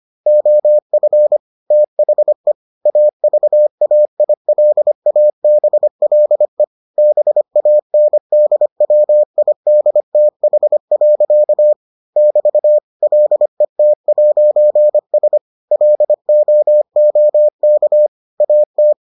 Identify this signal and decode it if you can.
Morse code, heard as a keyed tone, about 25 wpm, 600 Hz. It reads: OF THE AVAILABLE BANDWIDTH. <BT> LET'S LOOK AT